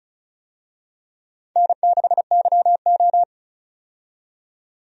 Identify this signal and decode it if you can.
Morse code, heard as a keyed tone, about 36 wpm, 700 Hz. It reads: N6YO
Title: USA Call Signs - 4 Characters - 35wpm